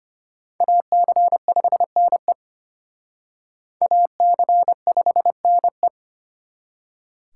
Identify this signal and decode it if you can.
Morse code, heard as a keyed tone, about 31 wpm, 700 Hz. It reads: AC5NE AC5NE
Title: Speed Racing - USA 2x2 Call Signs 25wpm